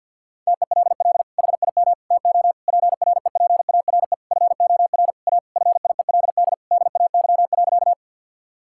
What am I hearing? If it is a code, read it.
Morse code, heard as a keyed tone, about 50 wpm, 700 Hz. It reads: TELL HIM TO PREPARE FOR A FIELD DAY.
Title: Sentences from Top 1000 Words - Review Part 6 - 50wpm